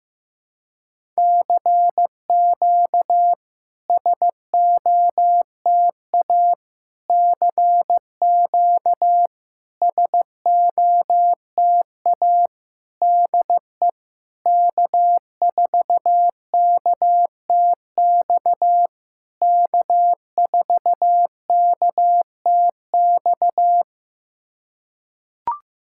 Morse code, heard as a keyed tone, about 15 wpm, 700 Hz. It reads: CQ SOTA CQ SOTA DE K4KTX K4KTX
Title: SOTA QSO Practice 15wpm